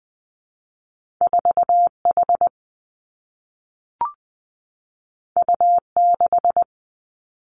Morse code, heard as a keyed tone, about 20 wpm, 700 Hz. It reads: 4 H U 6
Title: Two Challenging Letter-Number Combinations 20wpm